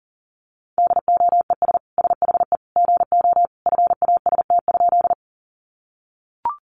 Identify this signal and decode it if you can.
Morse code, heard as a keyed tone, about 40 wpm, 700 Hz. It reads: DOES SHE GO FAST?
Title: Sentences from Top 400 Words 40wpm